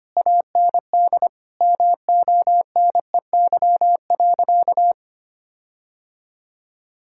Morse code, heard as a keyed tone, about 25 wpm, 700 Hz. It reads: AND MONEY.
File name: practice / 25wpm / Difficult Sentences with Hints - 25wpm